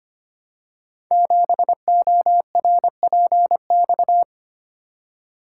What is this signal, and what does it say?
Morse code, heard as a keyed tone, about 25 wpm, 700 Hz. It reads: 7ORPX
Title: Code Groups - 5 Letter:Numbers - 25wpm